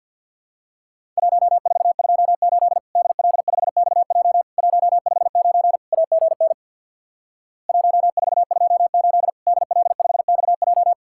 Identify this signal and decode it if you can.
Morse code, heard as a keyed tone, about 50 wpm, 700 Hz. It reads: 1428 DL5XJ 159 AGN 1428 DL5XJ
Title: Worked All Europe DX Contest - QTC Exchange with Repeats - 50wpm